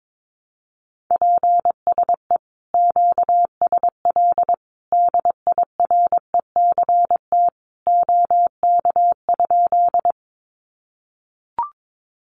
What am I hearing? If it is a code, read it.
Morse code, heard as a keyed tone, about 22 wpm, 700 Hz. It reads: PSE QSL DIRECT OK?
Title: Common QSO Phrases 22wpm